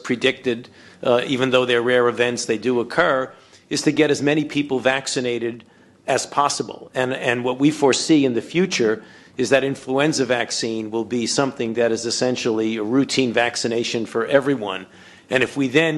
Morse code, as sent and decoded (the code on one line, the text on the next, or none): none